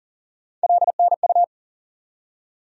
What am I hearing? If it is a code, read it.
Morse code, heard as a keyed tone, about 40 wpm, 700 Hz. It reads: LNU
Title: USA Call Sign Postfixes - 3 Letters 40wpm